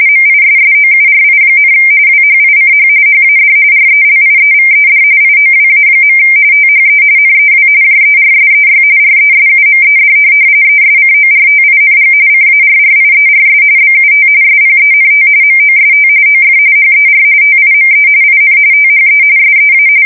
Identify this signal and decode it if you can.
RTTY, about 45.45 baud. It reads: MISSOURI, WHICH HAD RELIEVED IOWA. THE NEW JERSEY RESUMED ESCORTING CARRIERS BUT SPENT MOST OF HER TIME IN THE SEA
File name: fsk170 / USS-New-Jersey